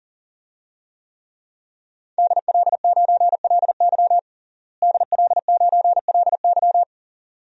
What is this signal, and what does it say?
Morse code, heard as a keyed tone, about 37 wpm, 700 Hz. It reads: DL9LY DL9LY
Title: DXCC-100 Call Signs - 5 Characters - Repeated 2x - 40wpm